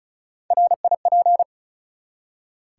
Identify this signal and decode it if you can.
Morse code, heard as a keyed tone, about 36 wpm, 700 Hz. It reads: RIP
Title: Three Letter Words - 35wpm